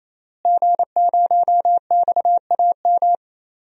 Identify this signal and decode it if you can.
Morse code, heard as a keyed tone, about 28 wpm, 700 Hz. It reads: G0XAM
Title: DXCC-100 Call Signs - 5 Characters - Repeated 3x - 28wpm